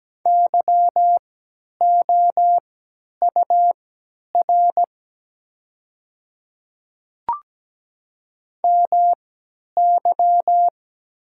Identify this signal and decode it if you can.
Morse code, heard as a keyed tone, about 17 wpm, spaced as for 10 wpm, 700 Hz. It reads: YOUR MY
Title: 3x Character Spacing - Top 100 Words - 17wpm